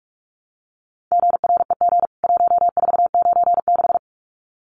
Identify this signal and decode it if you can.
Morse code, heard as a keyed tone, about 44 wpm, 700 Hz. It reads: GREG 1496
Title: CWT Exchange 45wpm